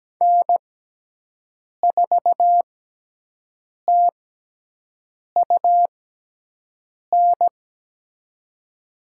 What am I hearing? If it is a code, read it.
Morse code, heard as a keyed tone, about 17 wpm, spaced as for 6 wpm, 700 Hz. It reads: N4TUN
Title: USA Call Signs - 6x Character Spacing - 17wpm